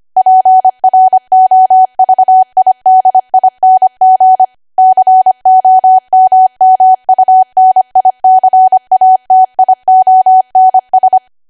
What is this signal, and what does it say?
Morse code, heard as a keyed tone, about 25 wpm, 750 Hz. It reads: PROVIDING COMMUNICATIONS
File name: Archive / 25 WPM / 25WPM